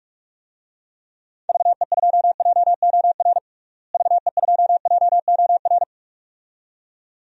Morse code, heard as a keyed tone, about 46 wpm, 700 Hz. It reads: VE2JOR VE2JOR
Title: DXCC-100 Call Signs - 6 Characters - Repeated 2x - 45wpm